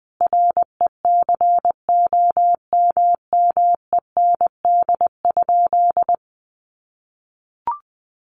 Morse code, heard as a keyed tone, about 20 wpm, 700 Hz. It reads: RECOMMEND?
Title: Sentences from Top 1200 Words 20wpm